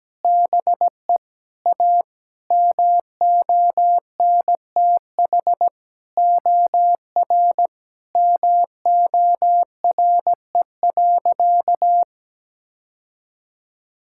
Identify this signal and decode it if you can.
Morse code, heard as a keyed tone, about 17 wpm, 700 Hz. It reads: BE A MONTH OR MORE.
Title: Sentences from Top 1000 Words - Review Part 5 - 17wpm